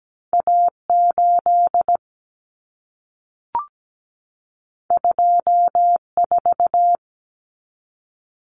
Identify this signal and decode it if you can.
Morse code, heard as a keyed tone, about 17 wpm, 700 Hz. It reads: A8 24
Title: Common Cut Numbers - 2-Digit 17wpm